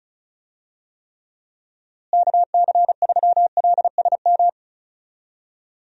Morse code, heard as a keyed tone, about 34 wpm, 700 Hz. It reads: KC3LSM
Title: USA Call Signs - 6 Characters - Repeated 2x - 35wpm